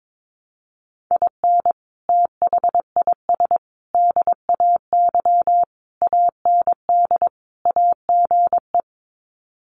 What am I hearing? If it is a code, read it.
Morse code, heard as a keyed tone, about 22 wpm, 700 Hz. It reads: IN THIS DAY AND AGE
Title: Sets of 5 Words Encore 22wpm